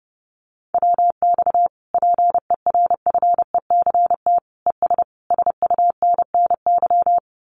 Morse code, heard as a keyed tone, about 30 wpm, 700 Hz. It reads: WX PERFECT ES SUNNY
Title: Common QSO Phrases Encore 30wpm